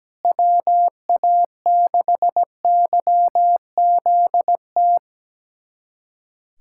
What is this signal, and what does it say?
Morse code, heard as a keyed tone, about 17 wpm, 700 Hz. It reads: WA6YZT
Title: USA Callsign DE Callsign - 6 Characters - 17wpm